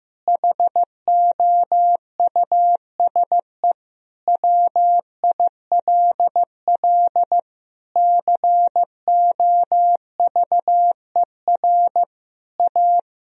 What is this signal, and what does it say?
Morse code, heard as a keyed tone, about 15 wpm, 700 Hz. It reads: HOUSE WILL COVER A
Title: Sentences from Top 1000 Words - Review Part 2 - 15wpm